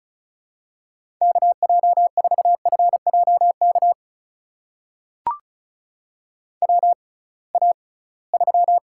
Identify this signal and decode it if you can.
Morse code, heard as a keyed tone, about 35 wpm, spaced as for 23 wpm, 700 Hz. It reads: KJ4FJK WA3
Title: USA Call Signs - 6 Characters - 6x Character Spacing - 35wpm